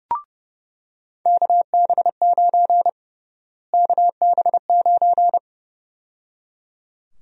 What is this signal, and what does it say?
Morse code, heard as a keyed tone, about 30 wpm, 700 Hz. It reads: KB9 KB9
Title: USA Call Sign Prefixes - Repeated 2x - 30wpm